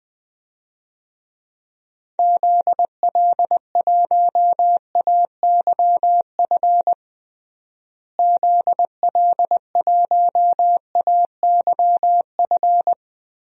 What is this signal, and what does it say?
Morse code, heard as a keyed tone, about 20 wpm, 700 Hz. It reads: ZL1AYF ZL1AYF
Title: DXCC-100 Call Signs - 6 Characters - Repeated 3x - 20wpm